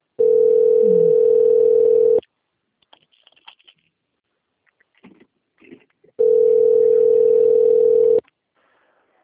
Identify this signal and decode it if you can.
Ringback tone (North America), 440 + 480 Hz.